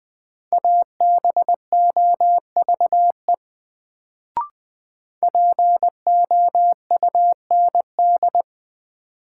Morse code, heard as a keyed tone, about 20 wpm, 700 Hz. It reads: ABOVE POUND
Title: Sentences from Top 1000 Words - Review Part 2 - New Word Drill - 20wpm